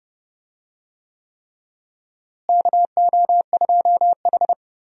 Morse code, heard as a keyed tone, about 32 wpm, 700 Hz. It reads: KO2H
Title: USA Call Signs - 4 Characters - Repeated 3x - 30wpm